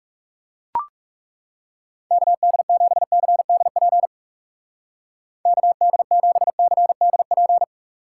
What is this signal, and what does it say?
Morse code, heard as a keyed tone, about 40 wpm, 700 Hz. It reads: KD7CDP KD7CDP